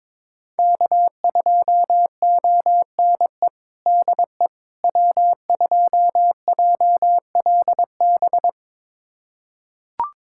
Morse code, heard as a keyed tone, about 22 wpm, 700 Hz. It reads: K2ONE DE W2JLB
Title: USA Callsign DE Callsign - 5 Characters - 22wpm